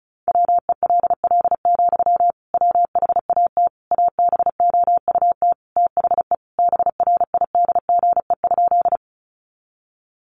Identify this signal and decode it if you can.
Morse code, heard as a keyed tone, about 34 wpm, 700 Hz. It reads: WELL, WHAT ABOUT THE BRIDGE?
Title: Sentences from Top 1200 Words 35wpm